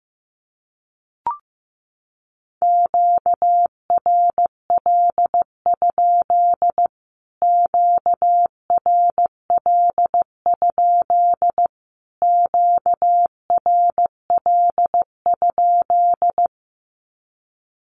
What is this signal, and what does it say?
Morse code, heard as a keyed tone, about 14 wpm, 700 Hz. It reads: QRL? QRL? QRL?
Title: Q-Codes Repeated 3x 15wpm